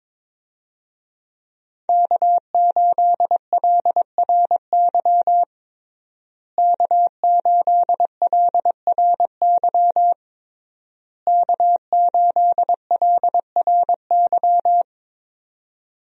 Morse code, heard as a keyed tone, about 22 wpm, 690 Hz. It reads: K8LRY K8LRY K8LRY
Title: USA Call Signs - 5 Characters - Repeated 3x - 22wpm